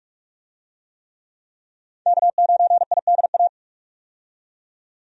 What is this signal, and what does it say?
Morse code, heard as a keyed tone, about 47 wpm, 700 Hz. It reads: K9IDA
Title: USA Call Signs - 5 Characters - 45wpm